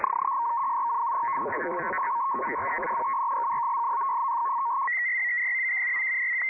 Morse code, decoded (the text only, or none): none